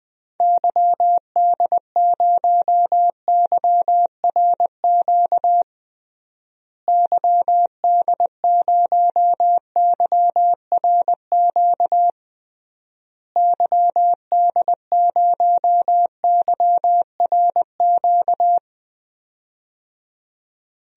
Morse code, heard as a keyed tone, about 21 wpm, 700 Hz. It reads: YD0YRQ YD0YRQ YD0YRQ
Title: DXCC-100 Call Signs - 6 Characters - Repeated 3x - 20wpm